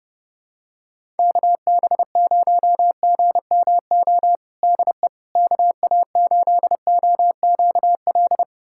Morse code, heard as a keyed tone, about 30 wpm, 700 Hz. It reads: KB0GMO DE KA8OQL
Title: USA Callsign DE Callsign 30wpm